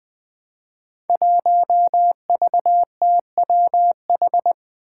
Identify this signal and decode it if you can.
Morse code, heard as a keyed tone, about 20 wpm, 700 Hz. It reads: 1VTWH
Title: Five Challenging Letter-Number Combinations 20wpm